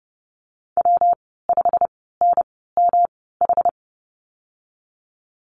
Morse code, heard as a keyed tone, about 30 wpm, spaced as for 17 wpm, 700 Hz. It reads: W5NMH
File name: Lesson 19g - Letter w - Call Signs 30wpm